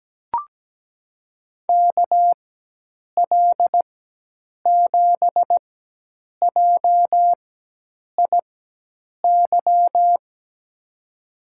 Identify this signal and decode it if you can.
Morse code, heard as a keyed tone, about 17 wpm, 700 Hz. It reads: K L 7 J I Y
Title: USA Call Signs - 6 Characters - 4x Character Spacing - 17wpm